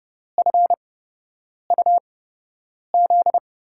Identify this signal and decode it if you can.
Morse code, heard as a keyed tone, about 30 wpm, spaced as for 8 wpm, 700 Hz. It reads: FUZ
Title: USA Call Signs - 6 Characters - 8x Character Spacing - 30wpm